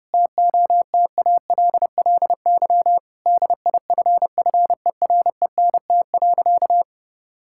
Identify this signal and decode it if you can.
Morse code, heard as a keyed tone, about 30 wpm, 700 Hz. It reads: TOTALLY DIFFERENT.
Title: Sentences from Top 1600 Words 30wpm